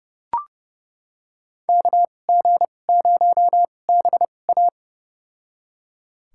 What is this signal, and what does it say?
Morse code, heard as a keyed tone, about 32 wpm, spaced as for 22 wpm, 700 Hz. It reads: KG0BA
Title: USA Call Signs - 2x Character Spacing - 30wpm